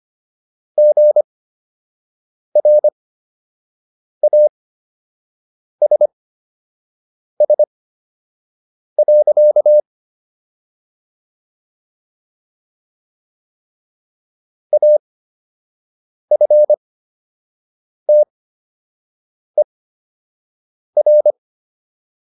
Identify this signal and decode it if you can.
Morse code, heard as a keyed tone, about 25 wpm, spaced as for 6 wpm, 600 Hz. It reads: GRASS. AFTER